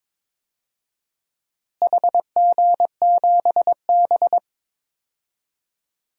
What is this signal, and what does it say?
Morse code, heard as a keyed tone, about 22 wpm, 700 Hz. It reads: HG7B
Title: DXCC-100 Call Signs - 4 Characters - Repeated 2x - 22wpm